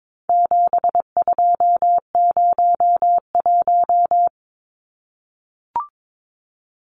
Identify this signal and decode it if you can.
Morse code, heard as a keyed tone, about 22 wpm, 700 Hz. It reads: 7201
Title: Numbers - 4-Digit 22wpm